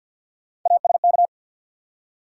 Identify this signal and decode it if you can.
Morse code, heard as a keyed tone, about 53 wpm, 700 Hz. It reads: ASK
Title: Three Letter Words - 50wpm